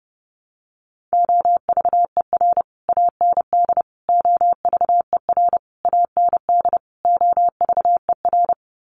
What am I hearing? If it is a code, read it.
Morse code, heard as a keyed tone, about 30 wpm, 700 Hz. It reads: OVER AND OVER AND OVER
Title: Sets of 5 Words Encore 30wpm